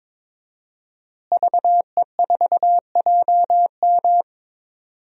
Morse code, heard as a keyed tone, about 22 wpm, 700 Hz. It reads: VE4JM
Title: DXCC-100 Call Signs - 5 Characters - Repeated 3x - 22wpm